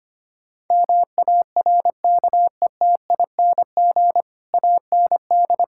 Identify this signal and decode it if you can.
Morse code, heard as a keyed tone, about 25 wpm, 700 Hz. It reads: MARKETING AND
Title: Sentences from Top 1000 Words - Review Part 5 - 25wpm